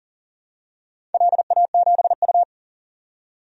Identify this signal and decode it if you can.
Morse code, heard as a keyed tone, about 38 wpm, 700 Hz. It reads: LA7U
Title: DXCC-100 Call Signs - 4 Characters - 40wpm